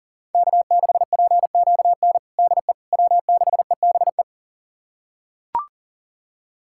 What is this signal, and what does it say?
Morse code, heard as a keyed tone, about 40 wpm, 690 Hz. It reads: K6PQN DE W6EBE
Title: USA Callsign DE Callsign - 5 Characters - 40wpm